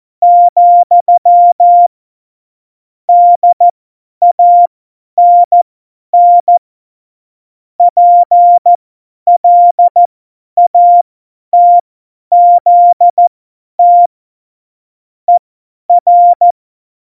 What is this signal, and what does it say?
Morse code, heard as a keyed tone, about 14 wpm, spaced as for 10 wpm, 700 Hz. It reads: , DANN PLATZT ER
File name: Morsekurs / Witze / Witz 25. Dezember, 50 F70